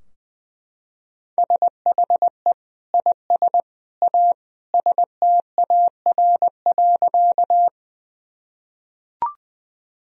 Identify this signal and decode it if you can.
Morse code, heard as a keyed tone, about 20 wpm, 700 Hz. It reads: SHE IS A STAR.
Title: Sentences from Top 1000 Words - Review Part 2 - 20wpm